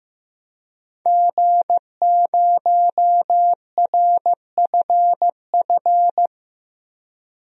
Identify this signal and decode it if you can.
Morse code, heard as a keyed tone, about 16 wpm, 700 Hz. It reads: G0RFF
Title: DXCC-100 Call Signs - 5 Characters - 15wpm